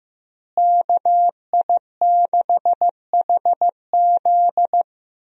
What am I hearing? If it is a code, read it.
Morse code, heard as a keyed tone, about 15 wpm, 700 Hz. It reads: KI6HZ
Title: USA Call Signs - 5 Characters - Repeated 3x - 15wpm